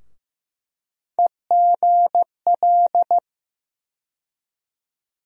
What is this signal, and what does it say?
Morse code, heard as a keyed tone, about 15 wpm, 700 Hz. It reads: EGL
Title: USA Call Sign Postfixes - 3 Letters 15wpm